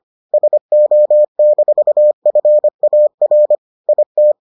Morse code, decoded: SO-FAR IT

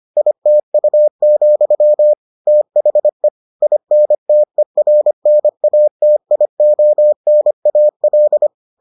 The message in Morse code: ITU, THE INTERNATIONAL

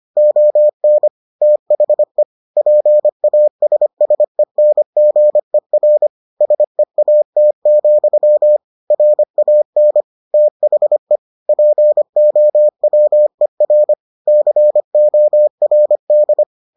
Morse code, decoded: ON THE PASSENGER SEAT, RAN THE POWER CORD

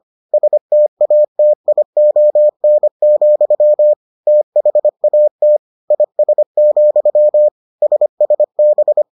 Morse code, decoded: STATION, THAT IS, SSB

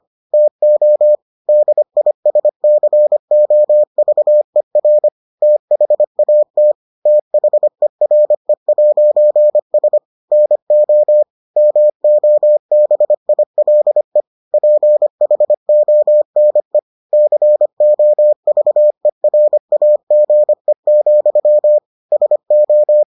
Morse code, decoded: TO DISCOVER THAT THERE'S NO MOBILE PHONE COVERAGE, SO

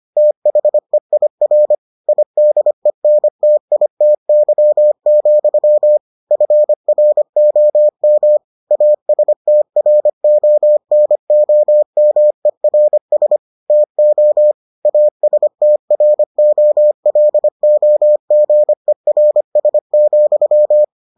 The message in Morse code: THEIR IDENTITY, FROM ASTRONOMERS TO ASTROLOGERS,

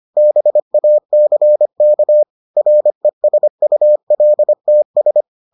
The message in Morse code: BACK RESULTS

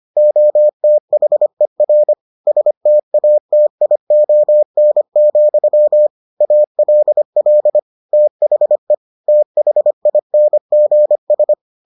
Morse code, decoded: OTHER STATION, ALL THE THINGS

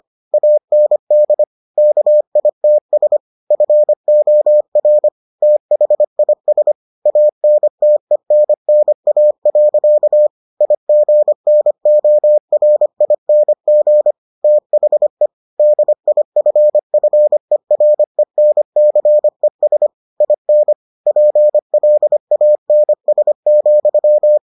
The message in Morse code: AND KITS FOR THIS ANTENNA. IGNORING THE DIFFERENCES IN PLANS,

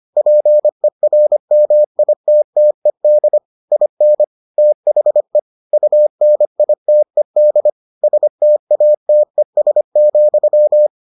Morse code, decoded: PERMITTED IN THE UNITED STATES,

